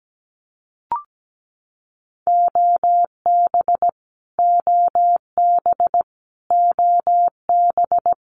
Morse code, heard as a keyed tone, about 17 wpm, 700 Hz. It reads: OB OB OB
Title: Abbreviations Repeated 3x - 17wpm